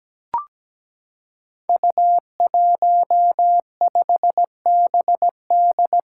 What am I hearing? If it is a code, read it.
Morse code, heard as a keyed tone, about 17 wpm, 700 Hz. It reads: U15BD
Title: Five Challenging Letter-Number Combinations 17wpm